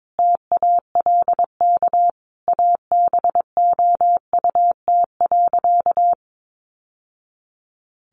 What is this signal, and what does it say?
Morse code, heard as a keyed tone, about 22 wpm, 700 Hz. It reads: TALK ABOUT.
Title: Sentences from Top 800 Words 22wpm